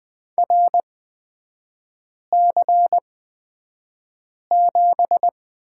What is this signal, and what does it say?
Morse code, heard as a keyed tone, about 20 wpm, spaced as for 5 wpm, 700 Hz. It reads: RC7
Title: 250 Lesson 40e - BK - Three Characters 20wpm